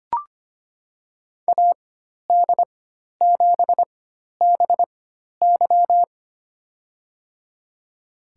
Morse code, heard as a keyed tone, about 25 wpm, spaced as for 12 wpm, 700 Hz. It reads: AD7BY
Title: USA Call Signs - 5 Characters - 4x Character Spacing - 25wpm